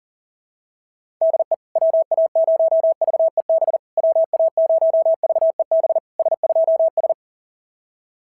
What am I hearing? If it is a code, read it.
Morse code, heard as a keyed tone, about 40 wpm, 690 Hz. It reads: DE WA0VEB WA0VEB S2S
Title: SOTA QSO Practice 40wpm